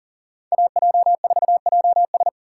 Morse code, heard as a keyed tone, about 40 wpm, 700 Hz. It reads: AJ4JS
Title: USA Call Signs - 5 Characters - 2x Character Spacing - 40wpm